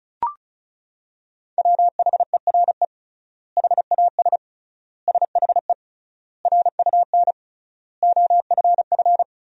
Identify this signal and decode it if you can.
Morse code, heard as a keyed tone, about 36 wpm, 700 Hz. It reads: WHERE HAS SHE RUN OFF
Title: 3x Word-Spacing - Sentences from Top 300 Words - 35wpm